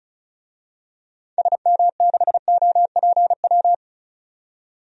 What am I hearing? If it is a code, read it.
Morse code, heard as a keyed tone, about 36 wpm, 690 Hz. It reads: SM6OPW
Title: DXCC-100 Call Signs - 6 Characters - 35wpm